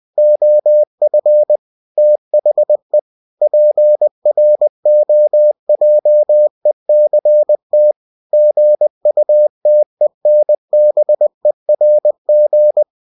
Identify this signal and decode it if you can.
Morse code, heard as a keyed tone, about 20 wpm, 600 Hz. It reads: OF THE PROJECT GUTENBERG
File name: cw-books / winnie-the-pooh / winnie-20wpm-page-0111